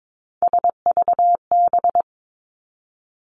Morse code, heard as a keyed tone, about 22 wpm, 700 Hz. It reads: SVB
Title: Three Challenging Letter-Number Combinations 22wpm